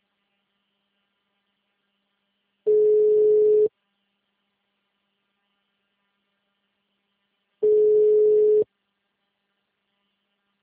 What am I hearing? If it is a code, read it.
Ringback tone (Europe, 425 Hz).
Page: Pirassununga